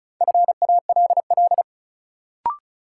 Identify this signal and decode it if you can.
Morse code, heard as a keyed tone, about 34 wpm, 700 Hz. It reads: FALL
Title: Sentences from Top 500 Words - New Word Drill - 35wpm